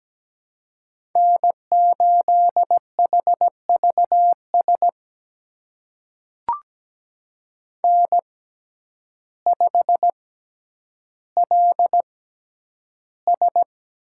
Morse code, heard as a keyed tone, about 17 wpm, spaced as for 9 wpm, 700 Hz. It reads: N8HVS N5LS
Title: USA Call Signs - 6x Character Spacing - 17wpm